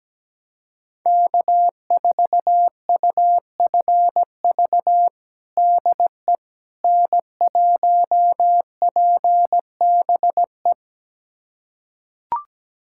Morse code, heard as a keyed tone, about 17 wpm, 700 Hz. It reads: K4UFV DE N1PBE
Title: USA Callsign DE Callsign - 5 Characters - 17wpm